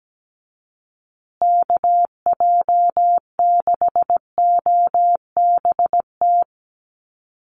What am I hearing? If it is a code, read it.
Morse code, heard as a keyed tone, about 17 wpm, 700 Hz. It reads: KJ6OBT
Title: USA Call Signs Repeated 3x 17wpm